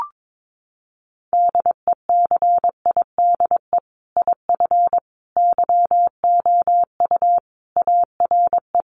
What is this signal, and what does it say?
Morse code, heard as a keyed tone, about 22 wpm, 700 Hz. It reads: DECIDE IF YOU ARE
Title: Sentences from Top 500 Words 22wpm